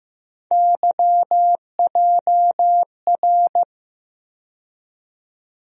Morse code, heard as a keyed tone, about 15 wpm, 700 Hz. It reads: YJR
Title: USA Call Sign Postfixes - 3 Letters 15wpm